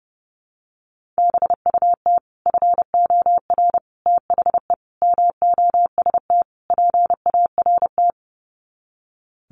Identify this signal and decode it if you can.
Morse code, heard as a keyed tone, about 30 wpm, 700 Hz. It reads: BUT FOR THE MOST PART
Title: Sets of 5 Words Encore 30wpm